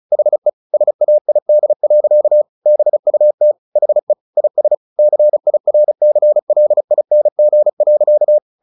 Morse code, heard as a keyed tone, about 34 wpm, 600 Hz. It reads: HE SAID. BUT HE IS CIRCLING.
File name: the old man and the sea 2 35wpm